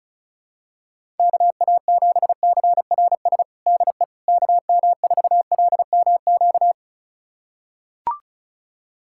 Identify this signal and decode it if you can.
Morse code, heard as a keyed tone, about 36 wpm, 700 Hz. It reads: KA7CRS DE KM4LMQ
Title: USA Callsign DE Callsign 35wpm